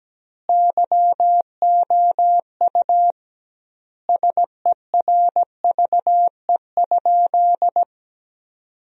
Morse code, heard as a keyed tone, about 17 wpm, 700 Hz. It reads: YOU SERVE?
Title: 2x Word-Spacing - Sentences from Top 400 Words - 17wpm